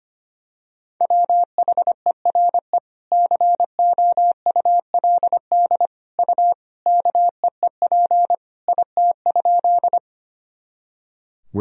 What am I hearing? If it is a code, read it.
Morse code, heard as a keyed tone, about 25 wpm, 700 Hz. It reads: WHERE COULD U KEEP IT?
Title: Sentences from Top 300 Words and Abbreviations 25wpm